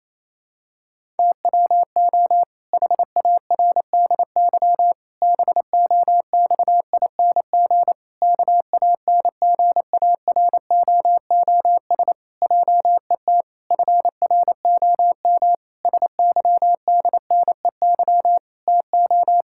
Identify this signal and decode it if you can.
Morse code, heard as a keyed tone, about 28 wpm, 700 Hz. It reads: TWO HARDY BOXING KANGAROOS JET FROM SYDNEY TO
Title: Pangram Challenge v2 - 28wpm